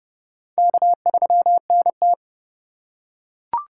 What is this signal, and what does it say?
Morse code, heard as a keyed tone, about 30 wpm, 700 Hz. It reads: K3NT
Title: USA Call Signs - 4 Characters - 2x Character Spacing - 30wpm